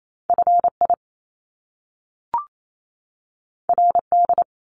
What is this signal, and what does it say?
Morse code, heard as a keyed tone, about 28 wpm, 700 Hz. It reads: FI RD
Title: Top Two Letter Combinations in Words 28wpm